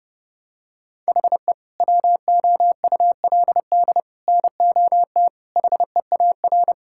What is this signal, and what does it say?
Morse code, heard as a keyed tone, about 29 wpm, 700 Hz. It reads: HE WOULD NOT HEAR
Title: Sentences from Top 1000 Words - Review Part 5 - 30wpm